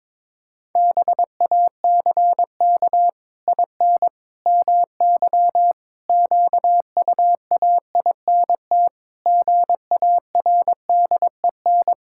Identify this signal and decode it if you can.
Morse code, heard as a keyed tone, about 22 wpm, 700 Hz. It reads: BACK IN MY QUAINT GARDEN
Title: Pangram Challenge v2 - 22wpm